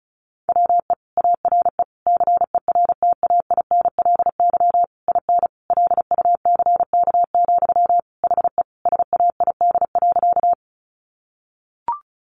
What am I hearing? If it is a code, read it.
Morse code, heard as a keyed tone, about 34 wpm, 700 Hz. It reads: WE ARE CERTAINLY IN LUCK, HE SAID.
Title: Sentences from Top 1500 Words 35wpm